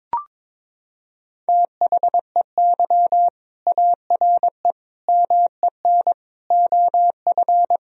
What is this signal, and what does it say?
Morse code, heard as a keyed tone, about 22 wpm, 700 Hz. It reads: THEY ARE MEN OF
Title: Sentences from Top 1000 Words - Review Part 2 - 22wpm